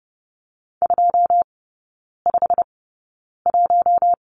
Morse code, heard as a keyed tone, about 30 wpm, spaced as for 9 wpm, 700 Hz. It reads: 251
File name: course / 30:10wpm / 087 Lesson 15e - Number 5 - Three Characters 30wpm